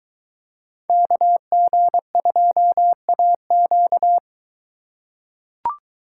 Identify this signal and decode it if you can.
Morse code, heard as a keyed tone, about 22 wpm, 700 Hz. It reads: KG2AQ